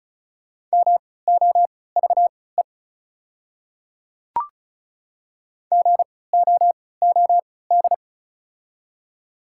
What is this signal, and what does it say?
Morse code, heard as a keyed tone, about 36 wpm, spaced as for 20 wpm, 700 Hz. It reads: MOVE GOOD
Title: 3x Character Spacing - Top 200 Words - 35wpm